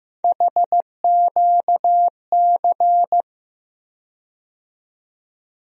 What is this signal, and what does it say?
Morse code, heard as a keyed tone, about 15 wpm, 700 Hz. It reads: HQC
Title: USA Call Sign Postfixes - 3 Letters 15wpm